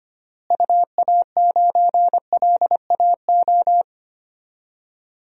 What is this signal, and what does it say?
Morse code, heard as a keyed tone, about 25 wpm, 700 Hz. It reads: UA9LAO
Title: DXCC-100 Call Signs - 6 Characters - Repeated 2x - 25wpm